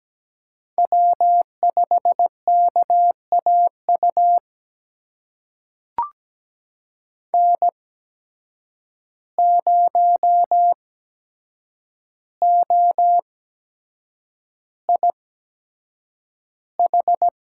Morse code, heard as a keyed tone, about 17 wpm, spaced as for 7 wpm, 700 Hz. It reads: W5KAU N0OIH
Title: USA Call Signs - 5 Characters - 8x Character Spacing - 17wpm